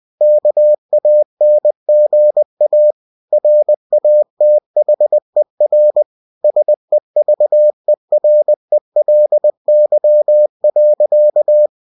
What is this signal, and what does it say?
Morse code, heard as a keyed tone, about 20 wpm, 600 Hz. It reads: KANGA RATHER SEVERELY.
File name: cw-books / winnie-the-pooh / winnie-20wpm-page-0072